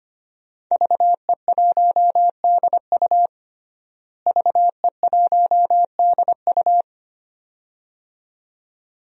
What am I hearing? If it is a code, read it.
Morse code, heard as a keyed tone, about 25 wpm, 700 Hz. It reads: VE1DU VE1DU
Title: DXCC-100 Call Signs - 5 Characters - Repeated 2x - 25wpm